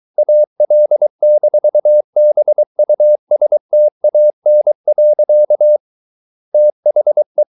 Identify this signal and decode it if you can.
Morse code, heard as a keyed tone, about 23 wpm, 600 Hz. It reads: AL-BUSTAN. THE